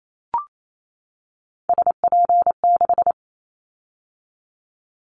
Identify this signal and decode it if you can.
Morse code, heard as a keyed tone, about 28 wpm, 700 Hz. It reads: SP6
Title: Callsign Prefixes for Top 25 DXCC - 28wpm